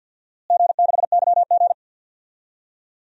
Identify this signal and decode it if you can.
Morse code, heard as a keyed tone, about 47 wpm, 700 Hz. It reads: G6YG
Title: DXCC-100 Call Signs - 4 Characters - Repeated 2x - 50wpm